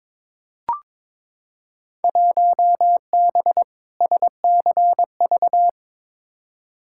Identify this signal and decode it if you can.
Morse code, heard as a keyed tone, about 22 wpm, 700 Hz. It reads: 1B SCV
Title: Field Day Exchanges - 22wpm